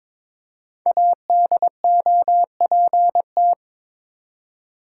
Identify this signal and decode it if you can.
Morse code, heard as a keyed tone, about 23 wpm, 700 Hz. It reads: ADOPT
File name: 5 Letter Words - First 3 Letters Given - 22wpm